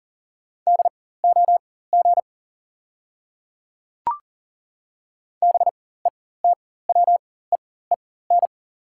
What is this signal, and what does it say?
Morse code, heard as a keyed tone, about 40 wpm, spaced as for 19 wpm, 700 Hz. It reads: DOG BETWEEN
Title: 4x Character Spacing - Top 500 Words - 40wpm